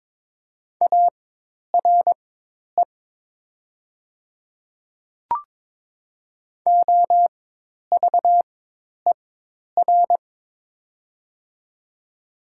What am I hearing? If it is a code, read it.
Morse code, heard as a keyed tone, about 22 wpm, spaced as for 10 wpm, 690 Hz. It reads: ARE OVER
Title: 4x Character Spacing - Top 100 Words - 22wpm